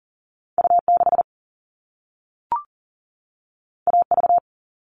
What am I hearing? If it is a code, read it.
Morse code, heard as a keyed tone, about 40 wpm, 700 Hz. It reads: U6 AV
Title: All Cut Numbers - 2-Digit 40wpm